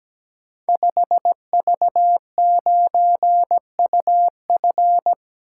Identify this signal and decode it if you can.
Morse code, heard as a keyed tone, about 17 wpm, 700 Hz. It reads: 5V9UF
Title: Code Groups - 5 Letter:Numbers - 17wpm